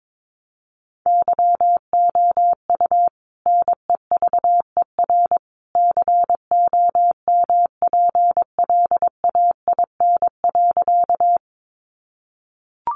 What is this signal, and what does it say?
Morse code, heard as a keyed tone, about 22 wpm, 700 Hz. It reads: YOU NEVER COMPLAIN.
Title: Sentences from Top 1300 Words 22wpm